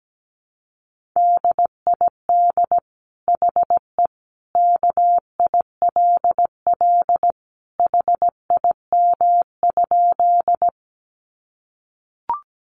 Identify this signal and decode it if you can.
Morse code, heard as a keyed tone, about 17 wpm, 700 Hz. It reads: DID HE KILL HIM?
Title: Sentences from Top 1400 Words 17wpm